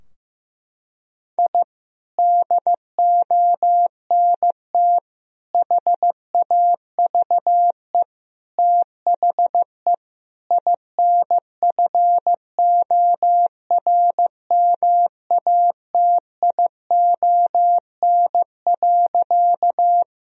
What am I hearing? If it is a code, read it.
Morse code, heard as a keyed tone, about 15 wpm, 700 Hz. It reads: I DONT HAVE THE INFORMATION.
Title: Sentences from Top 1000 Words - Review Part 3 - 15wpm